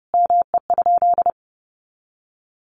Morse code, heard as a keyed tone, about 30 wpm, 700 Hz. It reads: ME?
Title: Sentences from Top 200 Words 30wpm